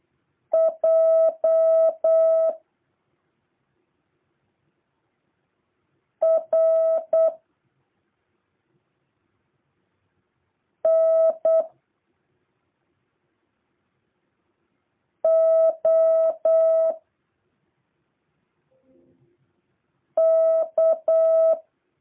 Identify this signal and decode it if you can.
Morse code, heard as a keyed tone, about 8 wpm, 650 Hz. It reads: J R N O K
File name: jh8cbh / 2023cw